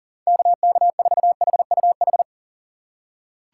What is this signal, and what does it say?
Morse code, heard as a keyed tone, about 40 wpm, 700 Hz. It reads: KK4HUH
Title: USA Call Signs - 2x Character Spacing - 40wpm